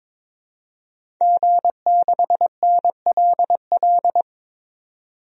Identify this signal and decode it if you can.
Morse code, heard as a keyed tone, about 22 wpm, 700 Hz. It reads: G6NLL
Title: DXCC-100 Call Signs - 5 Characters - 22wpm